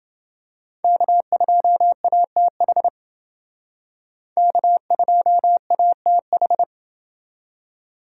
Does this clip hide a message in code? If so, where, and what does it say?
Yes: Morse code from 0.8 to 6.6 s, heard as a keyed tone, about 29 wpm, 700 Hz. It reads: K2ATH K2ATH